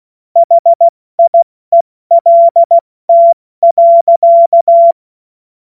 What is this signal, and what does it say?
Morse code, heard as a keyed tone, about 16 wpm, 680 Hz. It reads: HIELT.
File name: Morsekurs / Witze / Witz 05 Dezember 2023 70 F80